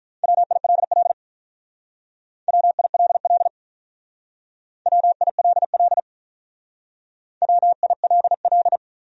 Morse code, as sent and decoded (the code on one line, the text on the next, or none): .-- .. .-.. .-.. / .-- .. .-.. .-.. / .-- .. .-.. .-.. / .-- .. .-.. .-..
WILL WILL WILL WILL